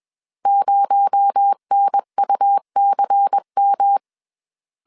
Morse code, heard as a keyed tone, about 21 wpm, 800 Hz. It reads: 0NUCM
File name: CW / DET / 1 6 1500C S B 20wpm 2048